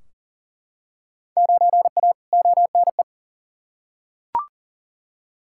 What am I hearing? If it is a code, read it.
Morse code, heard as a keyed tone, about 37 wpm, 700 Hz. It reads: 9A ONE
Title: Field Day Exchanges Encore - 40wpm